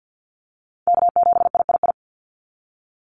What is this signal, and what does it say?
Morse code, heard as a keyed tone, about 50 wpm, 700 Hz. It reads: K7III
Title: NA SOTA Chaser Callsigns 50wpm